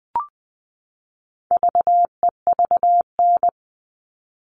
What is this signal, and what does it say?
Morse code, heard as a keyed tone, about 20 wpm, 700 Hz. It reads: VEVN
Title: All Cut Numbers - 4-Digit 20wpm